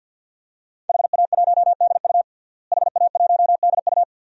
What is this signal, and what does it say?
Morse code, heard as a keyed tone, about 52 wpm, 700 Hz. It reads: HA1DU HA1DU
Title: DXCC-100 Call Signs - 5 Characters - Repeated 2x - 50wpm